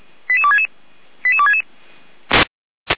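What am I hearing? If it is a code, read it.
Five-tone selective call (ZVEI), heard as quick runs of five tones, tones of about 70 ms. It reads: 88266 88266
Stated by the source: ZVEI five-tone selective call 88199 received on SDR, from Feuerwehr Orlamünde audio archive.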